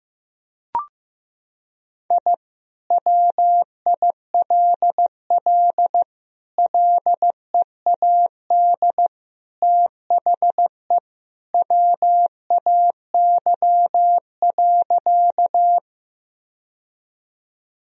Morse code, heard as a keyed tone, about 15 wpm, 700 Hz. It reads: I WILL LEAD THE WAY.
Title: Sentences from Top 1000 Words - Review Part 2 - 15wpm